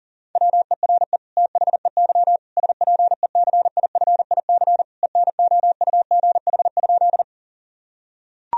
Morse code, heard as a keyed tone, about 40 wpm, 700 Hz. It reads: WERE THEY SPECIFIC ENOUGH?
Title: Sentences from Top 1000 Words - Review Part 3 - 40wpm